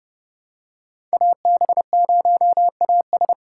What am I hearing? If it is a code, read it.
Morse code, heard as a keyed tone, about 30 wpm, 700 Hz. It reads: AB0AS
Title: USA Call Signs - 5 Characters - 6x Character Spacing - 30wpm